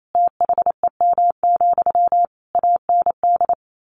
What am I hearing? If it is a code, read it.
Morse code, heard as a keyed tone, about 28 wpm, 700 Hz. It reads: THEM, AND
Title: A Sound Of Thunder 28wpm